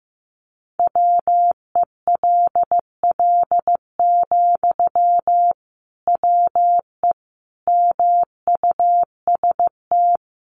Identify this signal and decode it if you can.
Morse code, heard as a keyed tone, about 15 wpm, 700 Hz. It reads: WELL, WE MUST
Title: Sentences from Top 1400 Words 15wpm